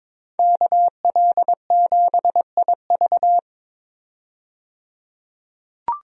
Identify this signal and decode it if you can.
Morse code, heard as a keyed tone, about 22 wpm, 700 Hz. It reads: KL7IV
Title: DXCC-100 Call Signs - 5 Characters - 22wpm